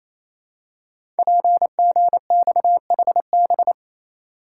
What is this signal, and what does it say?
Morse code, heard as a keyed tone, about 29 wpm, 700 Hz. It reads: PGXHB
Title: Code Groups - 5 Letters - 28wpm